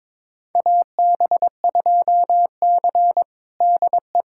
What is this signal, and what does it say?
Morse code, heard as a keyed tone, about 21 wpm, 700 Hz. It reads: AB2C DE
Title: USA Callsign DE Callsign - 4 Characters - 22wpm